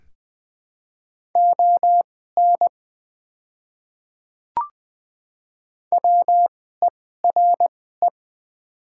Morse code, heard as a keyed tone, about 20 wpm, spaced as for 15 wpm, 700 Hz. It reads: ON WERE
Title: 2x Character Spacing - Top 100 Words - 20wpm